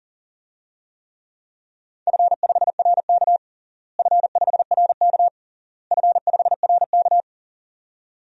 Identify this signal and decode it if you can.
Morse code, heard as a keyed tone, about 40 wpm, 700 Hz. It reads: F5RK F5RK F5RK
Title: DXCC-100 Call Signs - 4 Characters - Repeated 3x - 40wpm